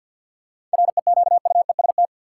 Morse code, heard as a keyed tone, about 53 wpm, 700 Hz. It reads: REQUEST
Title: Seven Letter Words - 50wpm